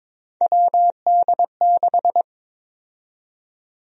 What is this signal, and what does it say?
Morse code, heard as a keyed tone, about 22 wpm, 700 Hz. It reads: WD6
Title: USA Call Sign Prefixes - 22wpm